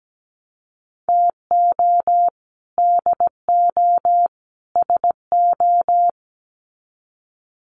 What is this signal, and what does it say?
Morse code, heard as a keyed tone, about 17 wpm, 700 Hz. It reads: TO DO SO
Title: Sets of 3 Words 17wpm